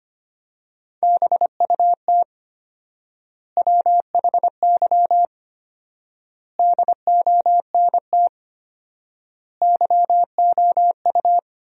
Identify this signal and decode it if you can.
Morse code, heard as a keyed tone, about 25 wpm, 700 Hz. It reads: BUT WHY DONT YOU
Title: 4x Word-Spacing - Sentences from Top 300 Words - 25wpm